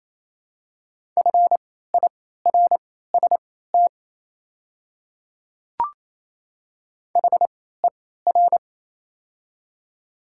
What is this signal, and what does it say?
Morse code, heard as a keyed tone, about 27 wpm, spaced as for 16 wpm, 700 Hz. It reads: FIRST HER
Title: 3x Character Spacing - Top 200 Words - 28wpm